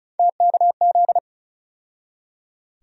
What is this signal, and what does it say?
Morse code, heard as a keyed tone, about 36 wpm, 700 Hz. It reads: TKZ
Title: USA Call Sign Postfixes - 3 Letters 35wpm